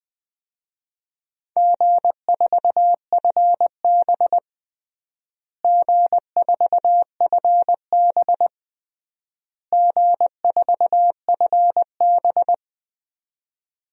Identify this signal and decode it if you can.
Morse code, heard as a keyed tone, about 20 wpm, 700 Hz. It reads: G4FB G4FB G4FB
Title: DXCC-100 Call Signs - 4 Characters - Repeated 3x - 20wpm